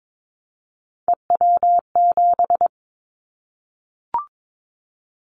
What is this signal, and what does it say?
Morse code, heard as a keyed tone, about 22 wpm, 700 Hz. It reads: EW7
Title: Callsign Prefixes for Top 100 DXCC - 22wpm